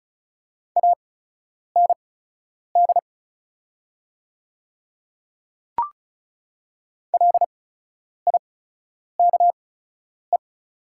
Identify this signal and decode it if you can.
Morse code, heard as a keyed tone, about 36 wpm, spaced as for 10 wpm, 700 Hz. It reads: AND LIKE
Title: 8x Character Spacing - Top 300 Words - 35wpm